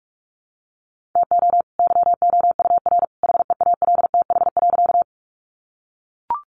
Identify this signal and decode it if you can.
Morse code, heard as a keyed tone, about 45 wpm, 700 Hz. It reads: TO YOUR HEALTH.
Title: Sentences from Top 600 Words 45wpm